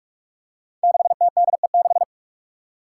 Morse code, heard as a keyed tone, about 44 wpm, 700 Hz. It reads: 6TDE6
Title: Five Challenging Letter-Number Combinations 45wpm